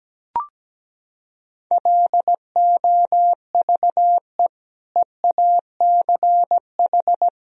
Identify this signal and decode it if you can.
Morse code, heard as a keyed tone, about 17 wpm, 700 Hz. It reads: LOVE EACH
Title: Sentences from Top 1000 Words - Review Part 1 - 17wpm